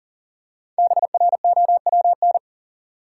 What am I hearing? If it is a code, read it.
Morse code, heard as a keyed tone, about 40 wpm, 700 Hz. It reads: BROWN
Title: Five Letter Words - 40wpm